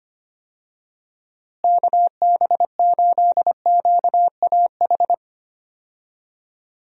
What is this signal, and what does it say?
Morse code, heard as a keyed tone, about 25 wpm, 700 Hz. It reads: KB8QAH
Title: USA Call Signs - 6 Characters - Repeated 2x - 25wpm